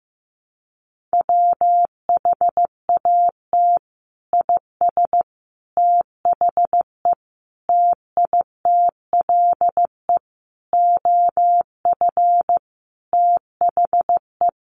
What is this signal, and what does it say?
Morse code, heard as a keyed tone, about 15 wpm, 700 Hz. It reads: WHAT IS THE TITLE OF THE
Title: Sentences from Top 1200 Words 15wpm